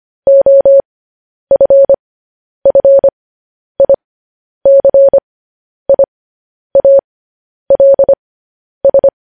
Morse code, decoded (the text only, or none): OFFICIALS